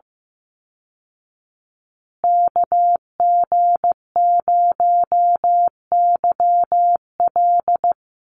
Morse code, heard as a keyed tone, about 15 wpm, 700 Hz. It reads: KG0YL
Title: Call Signs Repeated 2x - 15wpm